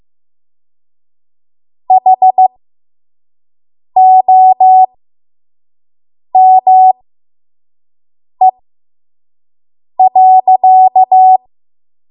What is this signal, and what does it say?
Morse code, heard as a keyed tone, about 15 wpm, spaced as for 5 wpm, 750 Hz. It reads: HOME.
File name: Morse / 05WPM